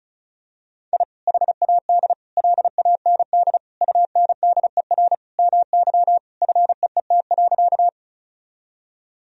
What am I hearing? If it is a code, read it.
Morse code, heard as a keyed tone, about 36 wpm, 700 Hz. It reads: I HAD LAND UNDER MY FEET.
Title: Sentences from Top 1000 Words - Review Part 1 - 35wpm